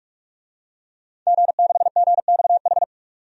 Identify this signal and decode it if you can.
Morse code, heard as a keyed tone, about 46 wpm, 700 Hz. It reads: G6GXH
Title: DXCC-100 Call Signs - 5 Characters - 45wpm